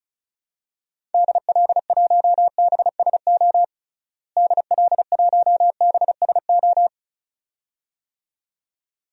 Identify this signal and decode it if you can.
Morse code, heard as a keyed tone, about 36 wpm, 700 Hz. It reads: DL1BSO DL1BSO
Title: DXCC-100 Call Signs - 6 Characters - Repeated 2x - 35wpm